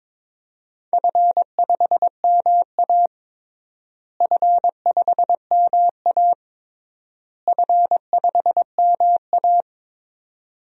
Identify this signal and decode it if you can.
Morse code, heard as a keyed tone, about 22 wpm, 700 Hz. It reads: F5MA F5MA F5MA
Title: DXCC-100 Call Signs - 4 Characters - Repeated 3x - 22wpm